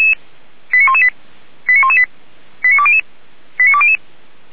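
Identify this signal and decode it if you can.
Five-tone selective call (ZVEI), heard as quick runs of five tones, tones of about 70 ms. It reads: 89118 89118 89200 89200
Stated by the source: ZVEI five-tone selective call 85730 received by SDR.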